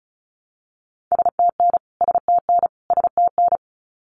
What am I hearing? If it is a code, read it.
Morse code, heard as a keyed tone, about 34 wpm, 700 Hz. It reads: STN STN STN
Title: Abbreviations Repeated 3x - 35wpm